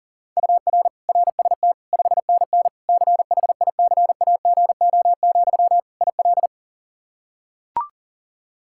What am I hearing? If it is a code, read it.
Morse code, heard as a keyed tone, about 40 wpm, 700 Hz. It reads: UR RST 5NN CHICAGO, IL
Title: K1USN SST QSO Practice 40wpm